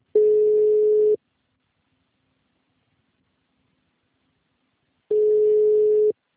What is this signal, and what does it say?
Ringback tone (Europe, 425 Hz).